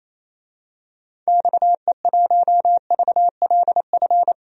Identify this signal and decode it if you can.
Morse code, heard as a keyed tone, about 28 wpm, 700 Hz. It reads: XE1VLF
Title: DXCC-100 Call Signs - 6 Characters - 28wpm